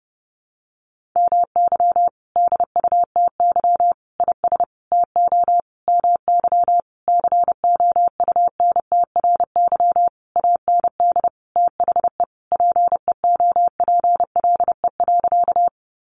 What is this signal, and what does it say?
Morse code, heard as a keyed tone, about 30 wpm, 700 Hz. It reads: MY DUTY IS TO MY COUNTRY AND THE PEOPLE.
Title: Sentences from Top 1500 Words 30wpm